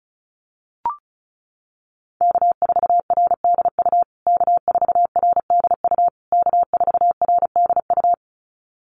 Morse code, heard as a keyed tone, about 34 wpm, 700 Hz. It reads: K4RDU K4RDU K4RDU
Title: USA Call Signs Repeated 3x 35wpm